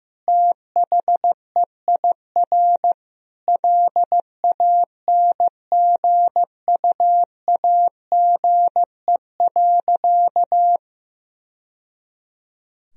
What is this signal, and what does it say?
Morse code, heard as a keyed tone, about 15 wpm, 700 Hz. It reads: THEIR LANGUAGE.
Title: Sentences from Top 1000 Words - Review Part 3 - 15wpm